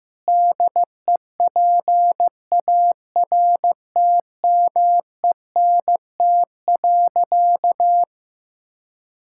Morse code, heard as a keyed tone, about 15 wpm, 700 Hz. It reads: DEPARTMENT.
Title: Sentences from Top 1000 Words - Review Part 5 - 15wpm